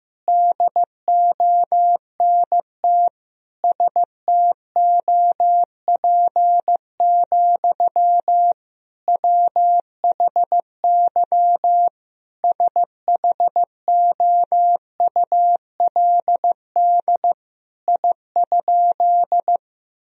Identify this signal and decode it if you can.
Morse code, heard as a keyed tone, about 15 wpm, 700 Hz. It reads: DONT STOP, WHY SHOULD I?
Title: Sentences from Top 1000 Words - Review Part 1 - 15wpm